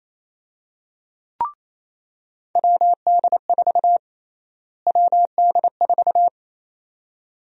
Morse code, heard as a keyed tone, about 28 wpm, 700 Hz. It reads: WD4 WD4
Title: USA Call Sign Prefixes - Repeated 2x - 28wpm